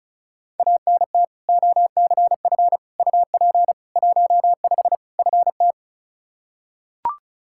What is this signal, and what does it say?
Morse code, heard as a keyed tone, about 36 wpm, 700 Hz. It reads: ANT OCF UP 15 FT
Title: Practice QSO - Exchange 2 - Info - 35wpm